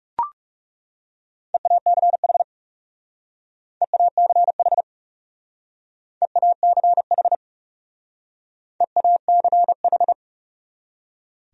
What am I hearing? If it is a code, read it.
Morse code, heard as a keyed tone, about 38 wpm, 700 Hz. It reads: EACH EACH EACH EACH